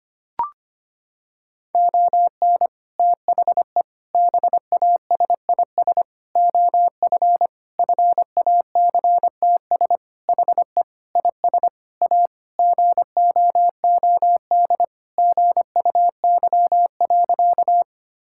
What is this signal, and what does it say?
Morse code, heard as a keyed tone, about 25 wpm, 700 Hz. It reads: ON THE BASIS OF FACTS HE IS A GOOD GUY.
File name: practice / 25wpm / Sentences with 3-Word Phrases - 25wpm